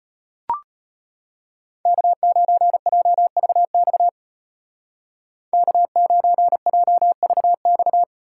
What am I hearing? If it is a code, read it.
Morse code, heard as a keyed tone, about 32 wpm, 700 Hz. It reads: K9JVX K9JVX